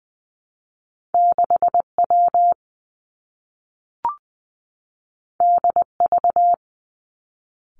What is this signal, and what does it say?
Morse code, heard as a keyed tone, about 20 wpm, 700 Hz. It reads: 6 W D V
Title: Two Challenging Letter-Number Combinations 20wpm